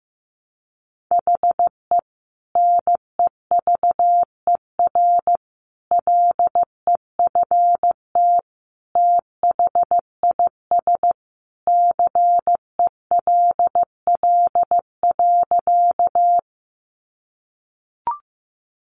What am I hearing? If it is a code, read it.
Morse code, heard as a keyed tone, about 15 wpm, 700 Hz. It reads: HE NEVER LEFT THIS CELL.
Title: Sentences from Top 1100 Words 15wpm